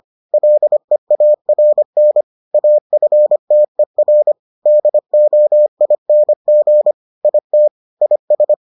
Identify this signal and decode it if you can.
Morse code, heard as a keyed tone, about 25 wpm, 600 Hz. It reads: LEARN AFTER DOING IT IS